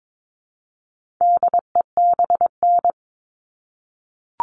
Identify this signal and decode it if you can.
Morse code, heard as a keyed tone, about 22 wpm, 700 Hz. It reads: DEBN
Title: All Cut Numbers - 4-Digit 22wpm